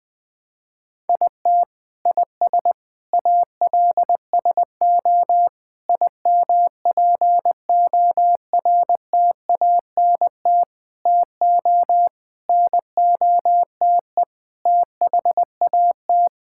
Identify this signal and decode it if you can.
Morse code, heard as a keyed tone, about 19 wpm, 700 Hz. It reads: IT IS ALSO IMPORTANT TO NOTE THAT
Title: Sets of 8 Words - 20wpm